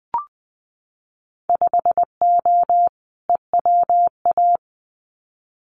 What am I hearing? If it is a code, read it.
Morse code, heard as a keyed tone, about 20 wpm, 700 Hz. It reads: 5O EWA
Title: Winter Field Day Exchanges Encore 20wpm